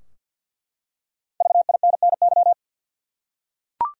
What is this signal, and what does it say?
Morse code, heard as a keyed tone, about 50 wpm, 700 Hz. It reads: VINNY
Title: USA Common Names - 50wpm